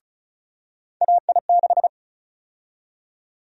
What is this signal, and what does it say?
Morse code, heard as a keyed tone, about 36 wpm, 700 Hz. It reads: AI6
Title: USA Call Sign Prefixes - 35wpm